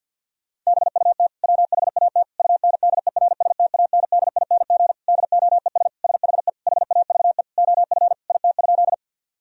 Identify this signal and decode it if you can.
Morse code, heard as a keyed tone, about 50 wpm, 700 Hz. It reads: BUT WHAT UNDERSTANDING DOES SHE HAVE OF IT?
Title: Sentences from Top 1000 Words - Review Part 3 - 50wpm